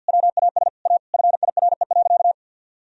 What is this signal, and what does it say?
Morse code, heard as a keyed tone, about 50 wpm, 700 Hz. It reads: WAS A FIRE.
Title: Sentences from Top 1000 Words - Review Part 4 - 50wpm